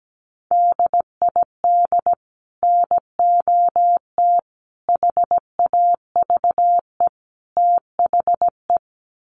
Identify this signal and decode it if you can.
Morse code, heard as a keyed tone, about 17 wpm, 700 Hz. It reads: DID NOT HAVE THE
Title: Sets of 4 Words 17wpm